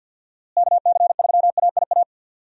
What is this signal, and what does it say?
Morse code, heard as a keyed tone, about 51 wpm, 700 Hz. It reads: KC3AIA
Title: USA Call Signs - 6 Characters - Repeated 3x - 50wpm